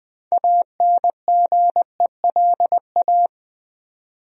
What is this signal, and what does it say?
Morse code, heard as a keyed tone, about 20 wpm, 700 Hz. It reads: ANGELA
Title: USA Common Names - 20wpm